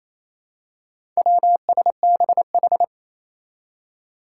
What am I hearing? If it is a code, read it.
Morse code, heard as a keyed tone, about 28 wpm, 700 Hz. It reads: WSBH
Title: Four Challenging Letter-Number Combinations 28wpm